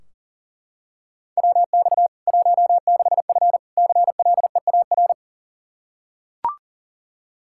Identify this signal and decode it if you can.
Morse code, heard as a keyed tone, about 40 wpm, 700 Hz. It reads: WX 16F CLEAR
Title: Practice QSO - Exchange 2 - Info - 40wpm